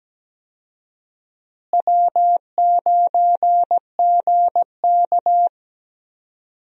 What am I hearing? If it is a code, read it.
Morse code, heard as a keyed tone, about 17 wpm, 700 Hz. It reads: W9GK
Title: USA Call Signs - 4 Characters - Repeated 3x - 17wpm